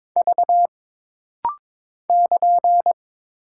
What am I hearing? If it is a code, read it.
Morse code, heard as a keyed tone, about 22 wpm, 700 Hz. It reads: V <KN>
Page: This rapid-fire timing is geared to push your ICR (Instant Character Recognition) skills.